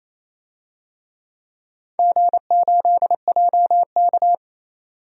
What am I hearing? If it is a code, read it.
Morse code, heard as a keyed tone, about 28 wpm, 700 Hz. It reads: G8JK
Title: DXCC-100 Call Signs - 4 Characters - 28wpm